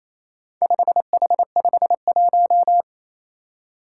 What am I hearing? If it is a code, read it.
Morse code, heard as a keyed tone, about 28 wpm, 700 Hz. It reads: 5H51
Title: Four Challenging Letter-Number Combinations 28wpm